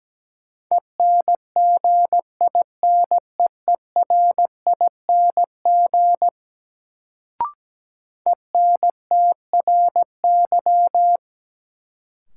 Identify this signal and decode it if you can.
Morse code, heard as a keyed tone, about 17 wpm, 700 Hz. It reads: ENGINEERING ENTRY
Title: Sentences from Top 1500 Words - New Word Drill - 17wpm